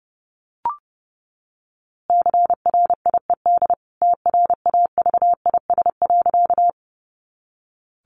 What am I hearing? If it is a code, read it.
Morse code, heard as a keyed tone, about 30 wpm, 700 Hz. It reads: CRIED TRAVIS.
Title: A Sound Of Thunder 30wpm